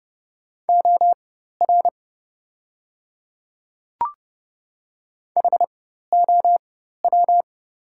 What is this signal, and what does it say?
Morse code, heard as a keyed tone, about 30 wpm, spaced as for 14 wpm, 700 Hz. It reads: OR HOW
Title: 4x Character Spacing - Top 300 Words - 30wpm